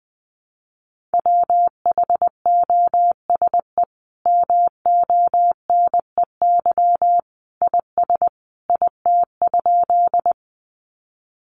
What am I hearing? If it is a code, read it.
Morse code, heard as a keyed tone, about 20 wpm, 700 Hz. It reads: WHOSE MONEY IS IT?
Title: Sentences from Top 1200 Words 20wpm